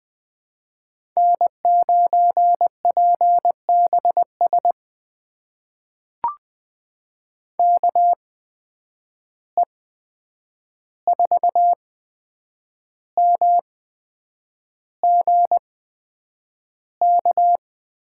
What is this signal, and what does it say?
Morse code, heard as a keyed tone, about 20 wpm, spaced as for 8 wpm, 700 Hz. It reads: N9PBS KE4MGK
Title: USA Call Signs - 8x Character Spacing - 20wpm